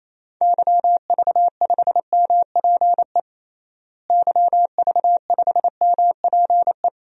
Morse code, heard as a keyed tone, about 28 wpm, 700 Hz. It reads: YV5MPE YV5MPE
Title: DXCC-100 Call Signs - 6 Characters - Repeated 3x - 28wpm